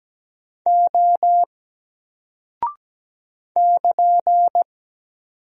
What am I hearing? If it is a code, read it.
Morse code, heard as a keyed tone, about 17 wpm, 700 Hz. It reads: O <KN>
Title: Single Character (Letter:Number:Punctuation:Prosign) Alt 17wpm